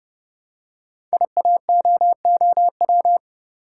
Morse code, heard as a keyed tone, about 29 wpm, 700 Hz. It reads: IAOOW
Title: Code Groups - 5 Letters - 30wpm